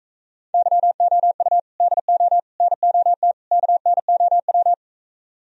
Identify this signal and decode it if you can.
Morse code, heard as a keyed tone, about 45 wpm, 700 Hz. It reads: YOU DO NOT KNOW